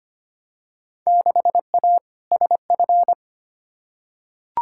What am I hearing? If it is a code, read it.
Morse code, heard as a keyed tone, about 25 wpm, 700 Hz. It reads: 6A SF
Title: Field Day Exchanges - 25wpm